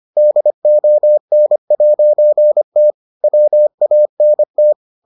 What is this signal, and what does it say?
Morse code, heard as a keyed tone, about 25 wpm, 600 Hz. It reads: DON'T WANT